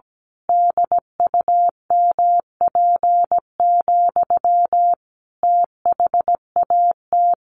Morse code, heard as a keyed tone, about 17 wpm, 700 Hz. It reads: DUMP, THAT
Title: A Sound Of Thunder 17wpm